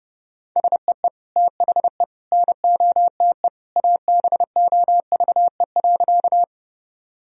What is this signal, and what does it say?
Morse code, heard as a keyed tone, about 30 wpm, 700 Hz. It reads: SEE THE NOTE ABOVE.
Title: Sentences from Top 1000 Words - Review Part 2 - 30wpm